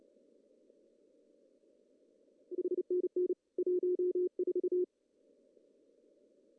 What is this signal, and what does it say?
Morse code, heard as a keyed tone, about 30 wpm, 400 Hz. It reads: HNN 14
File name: CW / cqww-cw-k5tr